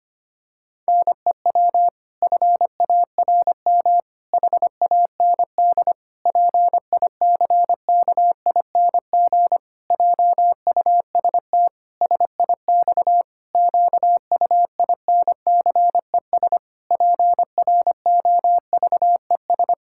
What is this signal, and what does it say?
Morse code, heard as a keyed tone, about 25 wpm, 700 Hz. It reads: NEW FARM HAND PICKING JUST SIX QUINCES PROVES
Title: Pangram Challenge v2 - 25wpm